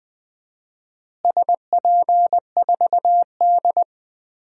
Morse code, heard as a keyed tone, about 20 wpm, 700 Hz. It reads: SP4D
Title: DXCC-100 Call Signs - 4 Characters - Repeated 3x - 20wpm